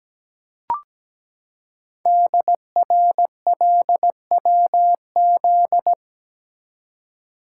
Morse code, heard as a keyed tone, about 17 wpm, 700 Hz. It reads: DRLWZ
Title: Code Groups - 5 Letters - 17wpm